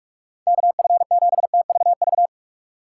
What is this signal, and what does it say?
Morse code, heard as a keyed tone, about 46 wpm, 700 Hz. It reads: KF7TVV
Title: USA Call Signs - 6 Characters - 2x Character Spacing - 45wpm